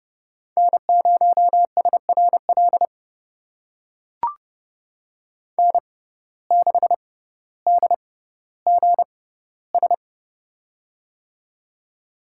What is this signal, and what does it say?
Morse code, heard as a keyed tone, about 30 wpm, spaced as for 15 wpm, 700 Hz. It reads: N0SRL N6DGS
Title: USA Call Signs - 5 Characters - 6x Character Spacing - 30wpm